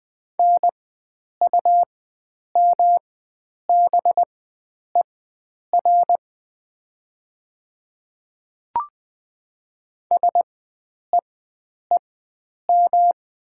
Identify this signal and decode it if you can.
Morse code, heard as a keyed tone, about 20 wpm, spaced as for 9 wpm, 700 Hz. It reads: NUMBER SEEM
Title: 4x Character Spacing - Top 500 Words - 20wpm